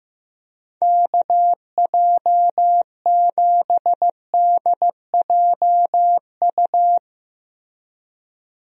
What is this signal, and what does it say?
Morse code, heard as a keyed tone, about 15 wpm, 700 Hz. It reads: KJ7DJU
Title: USA Call Signs - 6 Characters - Repeated 3x - 15wpm